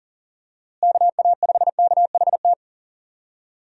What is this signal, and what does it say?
Morse code, heard as a keyed tone, about 40 wpm, 700 Hz. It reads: KA5KHT
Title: USA Call Signs - 6 Characters - Repeated 3x - 40wpm